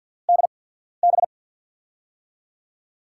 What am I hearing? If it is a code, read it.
Morse code, heard as a keyed tone, about 50 wpm, spaced as for 14 wpm, 700 Hz. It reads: DB
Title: USA Call Signs - 5 Characters - 8x Character Spacing - 50wpm